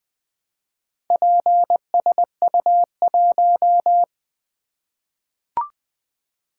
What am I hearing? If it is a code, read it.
Morse code, heard as a keyed tone, about 20 wpm, 700 Hz. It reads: PSU1
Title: Four Letter-Number Combinations - 20wpm